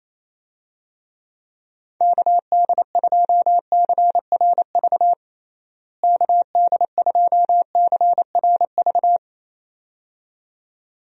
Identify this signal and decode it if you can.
Morse code, heard as a keyed tone, about 28 wpm, 700 Hz. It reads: KD2CRV KD2CRV
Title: USA Call Signs - 6 Characters - Repeated 2x - 28wpm